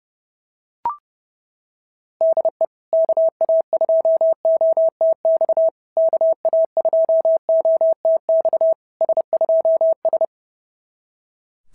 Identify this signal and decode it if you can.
Morse code, heard as a keyed tone, about 28 wpm, 660 Hz. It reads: DE KA2OTX KA2OTX S2S
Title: SOTA QSO Practice 30wpm